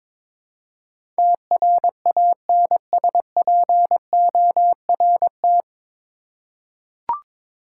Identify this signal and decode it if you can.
Morse code, heard as a keyed tone, about 22 wpm, 700 Hz. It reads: TRANSPORT
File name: practice / 22wpm / Nine Letter Words - 22wpm